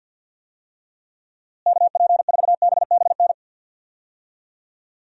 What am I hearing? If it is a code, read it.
Morse code, heard as a keyed tone, about 53 wpm, 700 Hz. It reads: KP4BBN
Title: DXCC-100 Call Signs - 6 Characters - Repeated 2x - 50wpm